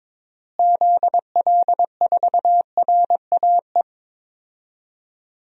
Morse code, heard as a keyed tone, about 22 wpm, 700 Hz. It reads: ZL4RAE
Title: DXCC-100 Call Signs - 6 Characters - Repeated 3x - 22wpm